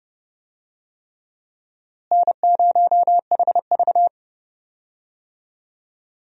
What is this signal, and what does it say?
Morse code, heard as a keyed tone, about 32 wpm, 700 Hz. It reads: N0HV
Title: USA Call Signs - 4 Characters - Repeated 3x - 30wpm